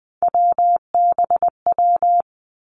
Morse code, heard as a keyed tone, about 20 wpm, 700 Hz. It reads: WBW
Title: Three Challenging Letter-Number Combinations 20wpm